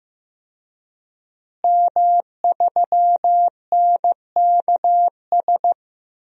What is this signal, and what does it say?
Morse code, heard as a keyed tone, about 15 wpm, 700 Hz. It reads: M3NKS
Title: DXCC-100 Call Signs - 5 Characters - Repeated 3x - 15wpm